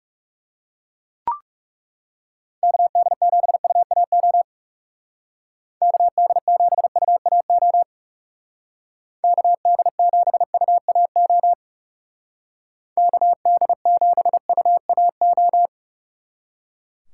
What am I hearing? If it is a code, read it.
Morse code, heard as a keyed tone, about 37 wpm, 700 Hz. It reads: KD7UAO KD7UAO KD7UAO KD7UAO
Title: Speed Racing - USA 2x3 Call Signs 30wpm